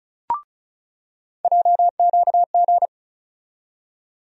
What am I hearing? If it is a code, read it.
Morse code, heard as a keyed tone, about 34 wpm, 700 Hz. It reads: JQG
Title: USA Call Sign Postfixes - 3 Letters 35wpm